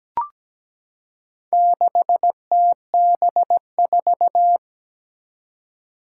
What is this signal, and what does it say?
Morse code, heard as a keyed tone, about 17 wpm, 700 Hz. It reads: 6TB4
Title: Four Challenging Letter-Number Combinations 17wpm